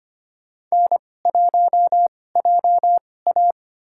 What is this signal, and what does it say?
Morse code, heard as a keyed tone, about 25 wpm, spaced as for 18 wpm, 700 Hz. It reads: N1JA
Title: USA Call Signs - 4 Characters - 2x Character Spacing - 25wpm